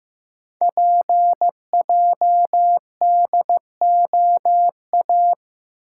Morse code, heard as a keyed tone, about 15 wpm, 690 Hz. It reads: PJDOA
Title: Code Groups - 5 Letters - 15wpm